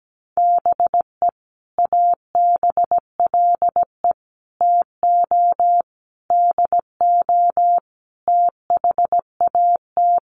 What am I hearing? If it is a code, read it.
Morse code, heard as a keyed tone, about 17 wpm, 700 Hz. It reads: BE ABLE TO DO THAT
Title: Sets of 5 Words Encore2 17wpm